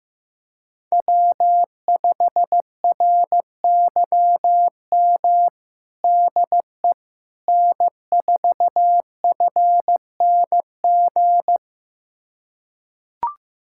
Morse code, heard as a keyed tone, about 16 wpm, 700 Hz. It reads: W5RYM DE N4FNG
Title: USA Callsign DE Callsign - 5 Characters - 15wpm